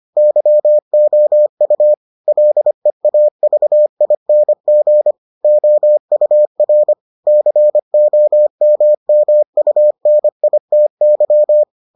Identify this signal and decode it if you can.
Morse code, heard as a keyed tone, about 25 wpm, 600 Hz. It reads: YOU LEAVING OUR COMMUNITY